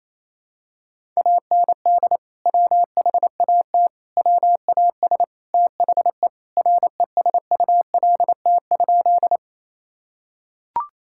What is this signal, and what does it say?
Morse code, heard as a keyed tone, about 28 wpm, 700 Hz. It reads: AND WHAT WAS THE RESULT?
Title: Sentences from Top 1000 Words - Review Part 5 - 28wpm